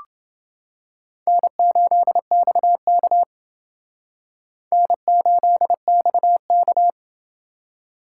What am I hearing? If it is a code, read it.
Morse code, heard as a keyed tone, about 28 wpm, 700 Hz. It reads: N8XK N8XK